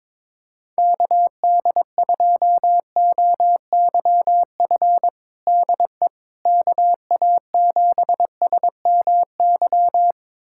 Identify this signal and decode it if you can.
Morse code, heard as a keyed tone, about 22 wpm, 700 Hz. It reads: KD2OYF DE KA7SMY
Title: USA Callsign DE Callsign - 6 Characters - 22wpm